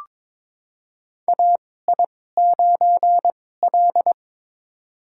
Morse code, heard as a keyed tone, about 22 wpm, spaced as for 16 wpm, 700 Hz. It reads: AI9L
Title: USA Call Signs - 2x Character Spacing - 22wpm